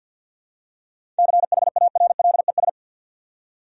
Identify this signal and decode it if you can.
Morse code, heard as a keyed tone, about 46 wpm, 700 Hz. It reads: CHARLES
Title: USA Common Names - 50wpm